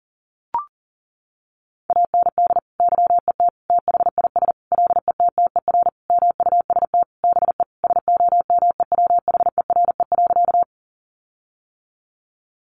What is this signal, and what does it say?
Morse code, heard as a keyed tone, about 40 wpm, 700 Hz. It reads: AND YET THIS LETTER MUST BE SOMEWHERE.
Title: Sentences from Top 1400 Words 40wpm